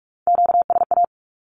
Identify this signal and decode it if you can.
Morse code, heard as a keyed tone, about 44 wpm, 700 Hz. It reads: QSA
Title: Q-Codes 45wpm